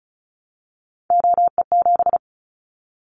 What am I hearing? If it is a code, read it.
Morse code, heard as a keyed tone, about 34 wpm, 700 Hz. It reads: OE7
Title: Callsign Prefixes for Top 25 DXCC - Repeated 3x - 35wpm